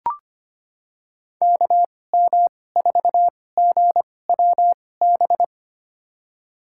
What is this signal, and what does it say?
Morse code, heard as a keyed tone, about 25 wpm, spaced as for 18 wpm, 700 Hz. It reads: KM4GWB
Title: USA Call Signs - 6 Characters - 2x Character Spacing - 25wpm